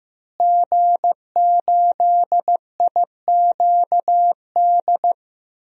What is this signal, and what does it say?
Morse code, heard as a keyed tone, about 15 wpm, 700 Hz. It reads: G8IQD
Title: DXCC-100 Call Signs - 5 Characters - Repeated 2x - 15wpm